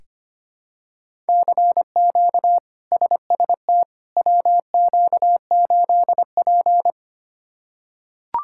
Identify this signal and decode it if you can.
Morse code, heard as a keyed tone, about 25 wpm, 700 Hz. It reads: CQ SST WQ8P
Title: K1USN SST QSO Practice - 25wpm